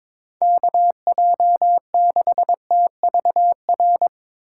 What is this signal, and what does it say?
Morse code, heard as a keyed tone, about 22 wpm, 700 Hz. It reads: KJ6TVR
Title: USA Call Signs - 6 Characters - 2x Character Spacing - 22wpm